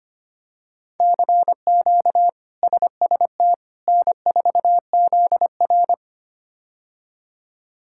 Morse code, heard as a keyed tone, about 25 wpm, 700 Hz. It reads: CQ SST N4ZR
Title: K1USN SST QSO Practice - 25wpm